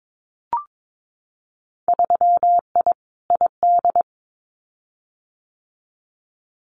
Morse code, heard as a keyed tone, about 22 wpm, 700 Hz. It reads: 3I ID
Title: Winter Field Day Exchanges 22wpm